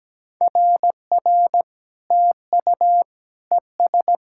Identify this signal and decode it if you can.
Morse code, heard as a keyed tone, about 17 wpm, 700 Hz. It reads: RR TU ES
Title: SOTA QSO Practice 17wpm